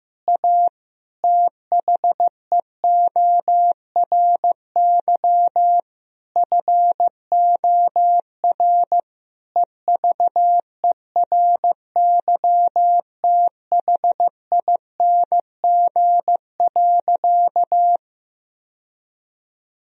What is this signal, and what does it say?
Morse code, heard as a keyed tone, about 16 wpm, 700 Hz. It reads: A THEORY FOR EVERYTHING.
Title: Sentences from Top 1000 Words - Review Part 3 - 15wpm